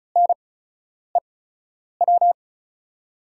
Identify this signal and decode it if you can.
Morse code, heard as a keyed tone, about 36 wpm, spaced as for 10 wpm, 700 Hz. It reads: NEW
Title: 8x Character Spacing - Top 500 Words - 35wpm